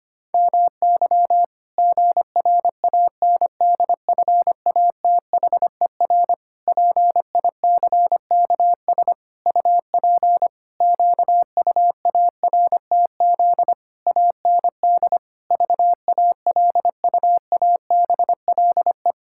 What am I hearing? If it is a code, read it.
Morse code, heard as a keyed tone, about 25 wpm, 700 Hz. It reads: MY GRANDFATHER PICKS UP QUARTZ AND VALUABLE
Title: Pangram Challenge v2 - 25wpm